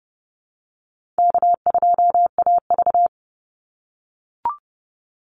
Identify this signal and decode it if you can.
Morse code, heard as a keyed tone, about 30 wpm, 700 Hz. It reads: K2AV
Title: Easy Call Signs - 30wpm